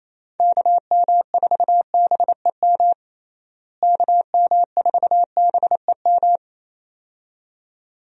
Morse code, heard as a keyed tone, about 28 wpm, 700 Hz. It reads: KM4BEM KM4BEM
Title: USA Call Signs - 6 Characters - Repeated 2x - 28wpm